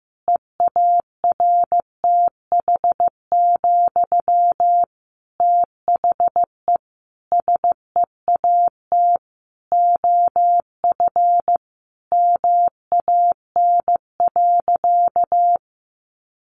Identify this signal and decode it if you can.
Morse code, heard as a keyed tone, about 15 wpm, 700 Hz. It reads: EARTH, THE SEAT OF MAN.
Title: Sentences from Top 1500 Words 15wpm